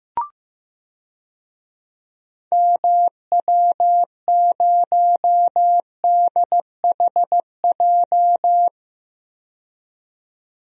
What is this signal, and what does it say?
Morse code, heard as a keyed tone, about 15 wpm, 700 Hz. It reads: MW0DHJ
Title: DXCC-100 Call Signs - 6 Characters - 15wpm